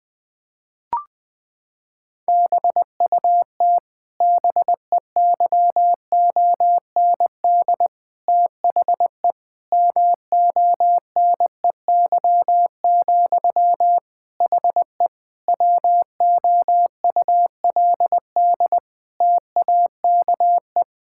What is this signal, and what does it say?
Morse code, heard as a keyed tone, about 20 wpm, 700 Hz. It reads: BUT BEYOND THE MONEY, HE WOULD TAKE
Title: Sentences from Top 1000 Words - Review Part 6 - 20wpm